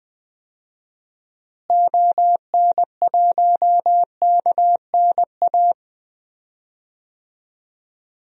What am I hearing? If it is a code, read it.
Morse code, heard as a keyed tone, about 20 wpm, 700 Hz. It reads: ON1KNA
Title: DXCC-100 Call Signs - 6 Characters - Repeated 3x - 20wpm